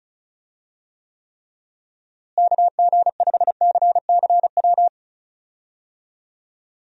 Morse code, heard as a keyed tone, about 34 wpm, 700 Hz. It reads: KG5CCW
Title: USA Call Signs - 6 Characters - Repeated 3x - 35wpm